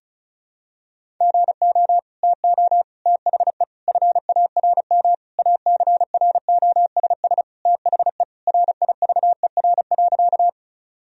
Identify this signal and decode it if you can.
Morse code, heard as a keyed tone, about 34 wpm, 700 Hz. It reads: GO TO THE FARM ACROSS THE RIVER.
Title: Sentences from Top 1000 Words - Review Part 2 - 35wpm